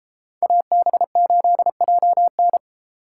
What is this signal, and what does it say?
Morse code, heard as a keyed tone, about 33 wpm, 700 Hz. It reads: AB8JN
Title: Speed Racing - USA 2x2 Call Signs 28wpm